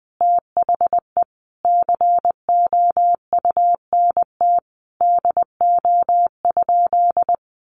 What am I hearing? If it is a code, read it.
Morse code, heard as a keyed tone, about 20 wpm, 700 Hz. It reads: THE COUNT DO?
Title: Sentences from Top 1500 Words 20wpm